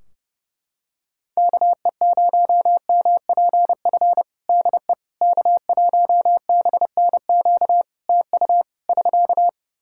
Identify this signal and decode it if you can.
Morse code, heard as a keyed tone, about 30 wpm, 700 Hz. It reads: KE0MPF DE K1BNQ TU <SK>
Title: Practice QSO - Exchange 4 - Ending - 30wpm